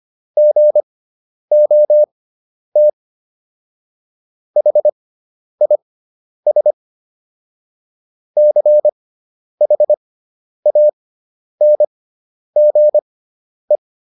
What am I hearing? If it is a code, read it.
Morse code, heard as a keyed tone, about 25 wpm, spaced as for 10 wpm, 600 Hz. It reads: GOT HIS CHANGE